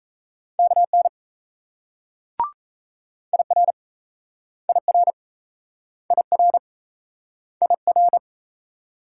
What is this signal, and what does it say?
Morse code, heard as a keyed tone, about 38 wpm, 700 Hz. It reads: KN IR IR IR IR